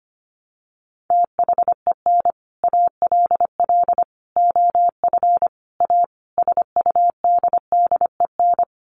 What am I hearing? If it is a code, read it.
Morse code, heard as a keyed tone, about 25 wpm, 700 Hz. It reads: THEN ALL OF A SUDDEN
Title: Sets of 5 Words Encore 25wpm